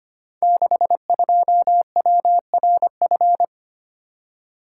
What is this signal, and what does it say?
Morse code, heard as a keyed tone, about 25 wpm, 700 Hz. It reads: 62WRF
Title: Code Groups - 5 Letter:Numbers - 25wpm